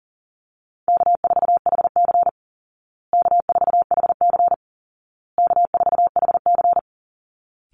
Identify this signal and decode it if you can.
Morse code, heard as a keyed tone, about 40 wpm, 700 Hz. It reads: K4HC K4HC K4HC
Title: Easy Call Signs Repeated 3x - 40wpm